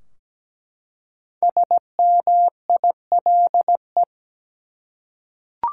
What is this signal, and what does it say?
Morse code, heard as a keyed tone, about 17 wpm, 700 Hz. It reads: SMILE
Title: Five Letter Words - 17wpm